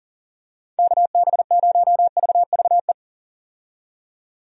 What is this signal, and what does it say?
Morse code, heard as a keyed tone, about 38 wpm, 700 Hz. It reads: KB0VVE
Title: USA Call Signs - 6 Characters - 40wpm